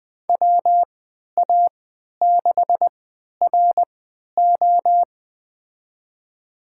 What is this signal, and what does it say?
Morse code, heard as a keyed tone, about 20 wpm, spaced as for 11 wpm, 700 Hz. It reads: WA6RO
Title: USA Call Signs - 3x Character Spacing - 20wpm